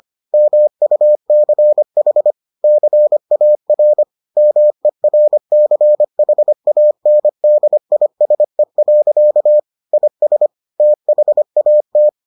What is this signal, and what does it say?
Morse code, heard as a keyed tone, about 25 wpm, 600 Hz. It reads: MUCH CAR MERCHANDISE. IS THAT